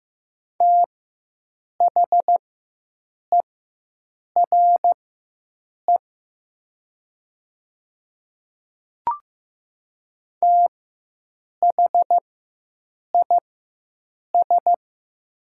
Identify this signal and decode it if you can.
Morse code, heard as a keyed tone, about 15 wpm, spaced as for 7 wpm, 700 Hz. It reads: THERE THIS
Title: 4x Character Spacing - Top 100 Words - 15wpm